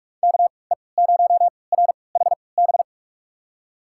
Morse code, heard as a keyed tone, about 46 wpm, spaced as for 26 wpm, 700 Hz. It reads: KE0RHB
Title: USA Call Signs - 6 Characters - 3x Character Spacing - 45wpm